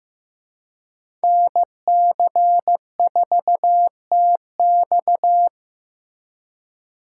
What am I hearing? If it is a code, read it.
Morse code, heard as a keyed tone, about 18 wpm, 700 Hz. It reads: NC4TX
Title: Speed Racing - USA 2x2 Call Signs 15wpm